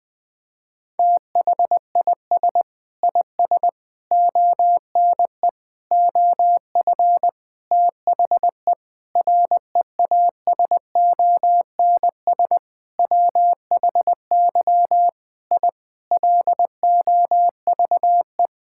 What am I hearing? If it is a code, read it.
Morse code, heard as a keyed tone, about 21 wpm, 700 Hz. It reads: THIS IS ONE OF THE REASONS WHY I LOVE
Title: Sentences with 7-Word Phrases - 20wpm